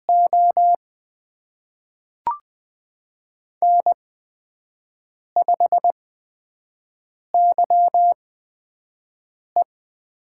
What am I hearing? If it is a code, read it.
Morse code, heard as a keyed tone, about 20 wpm, spaced as for 5 wpm, 700 Hz. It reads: O N5YE
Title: USA Call Signs - 5 Characters - 8x Character Spacing - 20wpm